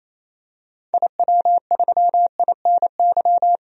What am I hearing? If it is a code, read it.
Morse code, heard as a keyed tone, about 28 wpm, 700 Hz. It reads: IW3INY
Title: DXCC-100 Call Signs - 6 Characters - 28wpm